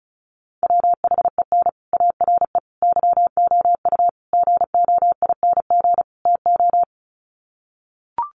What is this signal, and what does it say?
Morse code, heard as a keyed tone, about 34 wpm, 700 Hz. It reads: WHEN ARE YOU GOING TO
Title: Sets of 5 Words Encore2 35wpm